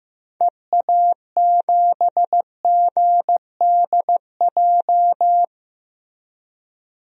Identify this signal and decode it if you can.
Morse code, heard as a keyed tone, about 15 wpm, 700 Hz. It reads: EA7GDJ
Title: DXCC-100 Call Signs - 6 Characters - Repeated 3x - 15wpm